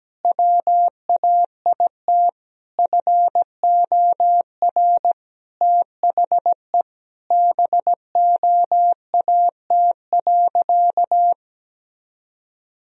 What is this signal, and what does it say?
Morse code, heard as a keyed tone, about 17 wpm, 700 Hz. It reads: WAIT FOR THE BOAT.
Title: Sentences from Top 1000 Words - Review Part 3 - 17wpm